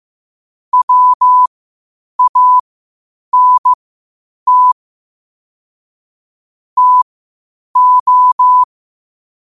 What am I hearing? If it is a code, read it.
Morse code, heard as a keyed tone, about 15 wpm, spaced as for 8 wpm, 1000 Hz. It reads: WANT TO
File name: qotdfiles / 8wpm